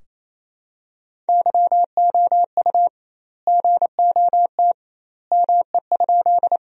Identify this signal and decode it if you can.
Morse code, heard as a keyed tone, about 28 wpm, 700 Hz. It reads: YOU GOT ME?
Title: 2x Word-Spacing - Sentences from Top 300 Words - 28wpm